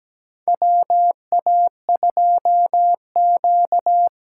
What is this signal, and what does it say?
Morse code, heard as a keyed tone, about 17 wpm, 700 Hz. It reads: WA2Q
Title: USA Call Signs - 6 Characters - 8x Character Spacing - 17wpm